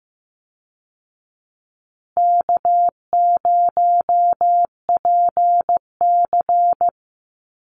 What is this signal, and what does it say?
Morse code, heard as a keyed tone, about 15 wpm, 700 Hz. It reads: K0PC
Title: Easy Call Signs Repeated 3x - 15wpm